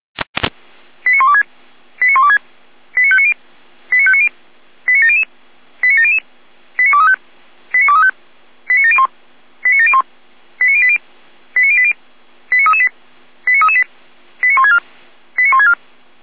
Five-tone selective call (ZVEI), heard as quick runs of five tones, tones of about 70 ms. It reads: 89126 89126 89590 89590 89700 89700 89235 89235 89701 89701 89080 89080 89338 89338 89164 89164